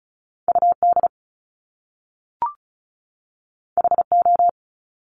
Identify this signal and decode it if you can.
Morse code, heard as a keyed tone, about 36 wpm, 700 Hz. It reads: UD HO
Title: Top Two Letter Combinations in Words 35wpm